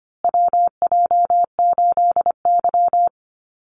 Morse code, heard as a keyed tone, about 25 wpm, 700 Hz. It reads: WJ8Y
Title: Call Signs Repeated 2x - 25wpm